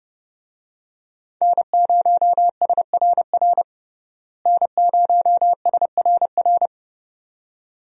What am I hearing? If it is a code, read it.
Morse code, heard as a keyed tone, about 30 wpm, 700 Hz. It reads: N0SRR N0SRR
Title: USA Call Signs - 5 Characters - Repeated 2x - 30wpm